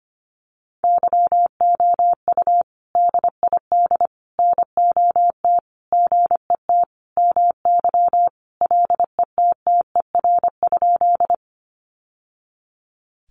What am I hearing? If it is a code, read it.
Morse code, heard as a keyed tone, about 25 wpm, 700 Hz. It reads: YOU DID NOT GET MY LETTER?
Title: Sentences from Top 600 Words 25wpm